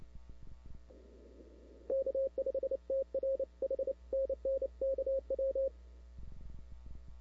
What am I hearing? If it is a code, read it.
Morse code, heard as a keyed tone, about 29 wpm, 540 Hz. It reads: K5TR H NN KW
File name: CW / 2005-k5tr arrl cw dx -00:00